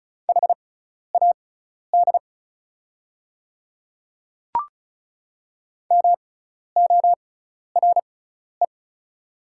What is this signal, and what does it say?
Morse code, heard as a keyed tone, about 36 wpm, spaced as for 12 wpm, 700 Hz. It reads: HAD MORE
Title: 6x Character Spacing - Top 100 Words - 35wpm